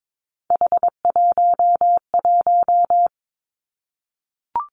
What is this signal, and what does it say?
Morse code, heard as a keyed tone, about 22 wpm, 700 Hz. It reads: H11
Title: Three Challenging Letter-Number Combinations 22wpm